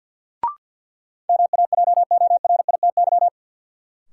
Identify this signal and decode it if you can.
Morse code, heard as a keyed tone, about 53 wpm, 700 Hz. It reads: MAJORITY
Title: Sets of 3 Words - New Word Drill - 50wpm